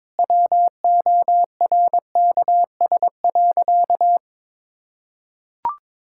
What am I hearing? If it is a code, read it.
Morse code, heard as a keyed tone, about 22 wpm, 700 Hz. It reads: WORKS.
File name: practice / 22wpm / Sentences with 2-Word Phrases - 22wpm